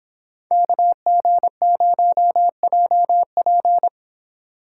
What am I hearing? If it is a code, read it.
Morse code, heard as a keyed tone, about 26 wpm, 700 Hz. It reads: KG0JP